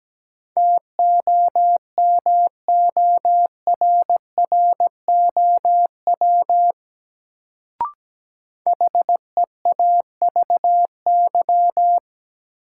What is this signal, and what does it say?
Morse code, heard as a keyed tone, about 17 wpm, 700 Hz. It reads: TOMORROW HEAVY
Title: Sentences from Top 1100 Words - New Word Drill Alt - 17wpm